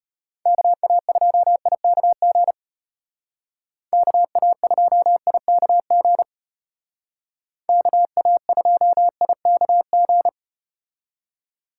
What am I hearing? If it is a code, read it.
Morse code, heard as a keyed tone, about 34 wpm, 700 Hz. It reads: KA2IKG KA2IKG KA2IKG